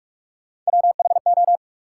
Morse code, heard as a keyed tone, about 46 wpm, 700 Hz. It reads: WHO
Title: Sets of 2 Words - New Word Drill - 45wpm